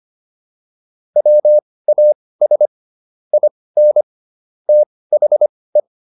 Morse code, heard as a keyed tone, about 25 wpm, spaced as for 18 wpm, 600 Hz. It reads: WAS IN THE
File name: WD204 18